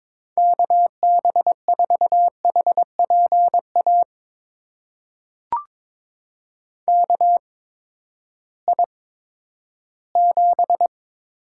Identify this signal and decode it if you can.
Morse code, heard as a keyed tone, about 21 wpm, spaced as for 12 wpm, 700 Hz. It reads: KB4HPA KI7
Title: USA Call Signs - 6 Characters - 8x Character Spacing - 22wpm